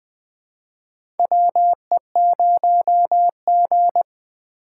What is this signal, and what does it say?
Morse code, heard as a keyed tone, about 21 wpm, 700 Hz. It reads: WE0G
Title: USA Call Signs - 4 Characters - 4x Character Spacing - 20wpm